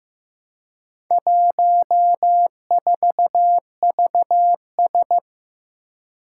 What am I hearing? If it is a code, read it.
Morse code, heard as a keyed tone, about 15 wpm, 700 Hz. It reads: 14VS
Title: Four Challenging Letter-Number Combinations 15wpm